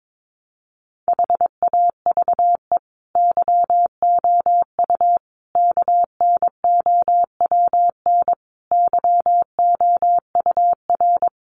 Morse code, heard as a keyed tone, about 22 wpm, 700 Hz. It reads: HAVE YOU KNOWN YOUR
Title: Sentences from Top 1300 Words 22wpm